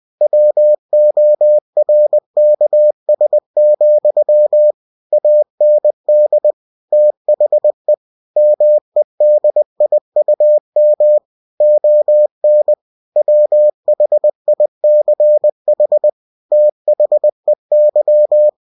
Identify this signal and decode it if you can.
Morse code, heard as a keyed tone, about 20 wpm, 600 Hz. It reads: WORKS, AND THE MEDIUM ON WHICH THEY